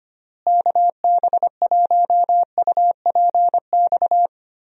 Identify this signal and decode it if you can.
Morse code, heard as a keyed tone, about 25 wpm, 700 Hz. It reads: KB1UPX
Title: USA Call Signs - 6 Characters - Repeated 3x - 25wpm